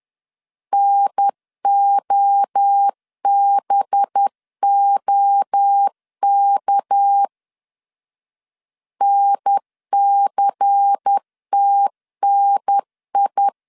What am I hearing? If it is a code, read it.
Morse code, heard as a keyed tone, about 10 wpm, 800 Hz. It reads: NOBOK NCTNI